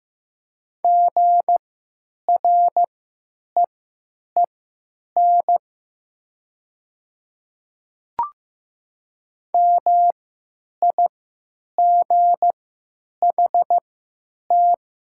Morse code, heard as a keyed tone, about 15 wpm, spaced as for 8 wpm, 700 Hz. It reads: GREEN MIGHT
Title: 3x Character Spacing - Top 500 Words - 15wpm